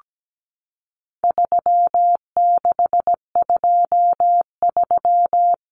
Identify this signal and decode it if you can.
Morse code, heard as a keyed tone, about 17 wpm, 700 Hz. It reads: 3623
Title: Numbers - 4-Digit 17wpm